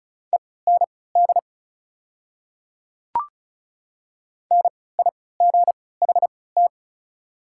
Morse code, heard as a keyed tone, about 34 wpm, spaced as for 20 wpm, 690 Hz. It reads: END NIGHT
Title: 3x Character Spacing - Top 300 Words - 35wpm